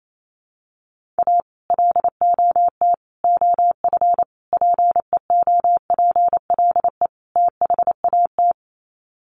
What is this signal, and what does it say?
Morse code, heard as a keyed tone, about 28 wpm, 700 Hz. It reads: A LOT OF PEOPLE THAT
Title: Sets of 5 Words Encore2 28wpm